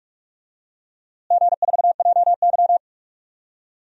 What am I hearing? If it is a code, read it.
Morse code, heard as a keyed tone, about 41 wpm, 690 Hz. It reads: G4JY
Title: DXCC-100 Call Signs - 4 Characters - Repeated 3x - 45wpm